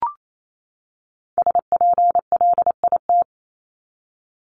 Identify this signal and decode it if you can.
Morse code, heard as a keyed tone, about 28 wpm, 700 Hz. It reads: SPLIT
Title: Top 1751-2000 Words 28wpm